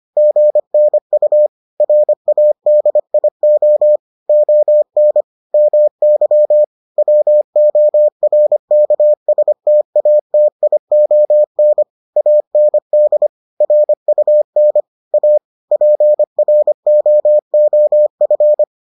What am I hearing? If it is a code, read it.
Morse code, heard as a keyed tone, about 25 wpm, 600 Hz. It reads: GNU RADIO ON MY WORKSTATION AND RUN A PROOF